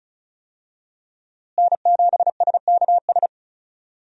Morse code, heard as a keyed tone, about 36 wpm, 700 Hz. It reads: N7SKS
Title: USA Call Signs - 5 Characters - Repeated 2x - 35wpm